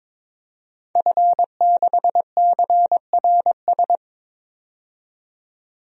Morse code, heard as a keyed tone, about 22 wpm, 700 Hz. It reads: F6CRS
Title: DXCC-100 Call Signs - 5 Characters - 22wpm